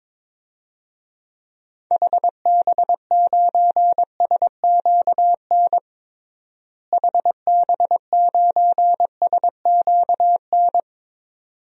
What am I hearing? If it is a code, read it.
Morse code, heard as a keyed tone, about 22 wpm, 700 Hz. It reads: HB9SQN HB9SQN
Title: DXCC-100 Call Signs - 6 Characters - Repeated 2x - 22wpm